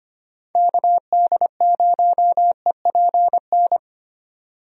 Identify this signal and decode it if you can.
Morse code, heard as a keyed tone, about 25 wpm, 700 Hz. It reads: KD0EPN
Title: USA Call Signs - 6 Characters - 25wpm